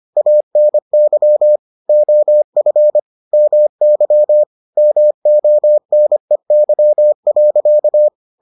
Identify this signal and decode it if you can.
Morse code, heard as a keyed tone, about 25 wpm, 600 Hz. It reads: ANY OF MY MONEY.